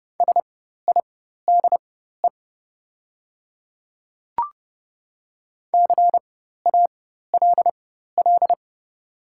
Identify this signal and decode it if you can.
Morse code, heard as a keyed tone, about 30 wpm, spaced as for 14 wpm, 690 Hz. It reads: SIDE CALL
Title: 4x Character Spacing - Top 100 Words - 30wpm